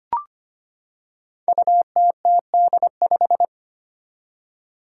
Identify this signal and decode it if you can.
Morse code, heard as a keyed tone, about 25 wpm, 700 Hz. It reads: UTTD5
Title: Five Challenging Letter-Number Combinations 25wpm